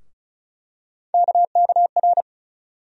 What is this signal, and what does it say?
Morse code, heard as a keyed tone, about 36 wpm, 700 Hz. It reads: KKR
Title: USA Call Sign Postfixes - 3 Letters 35wpm